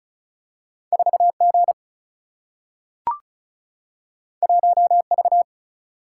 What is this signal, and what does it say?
Morse code, heard as a keyed tone, about 36 wpm, 700 Hz. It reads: 4G 1V
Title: Single-Letter Single-Number 35wpm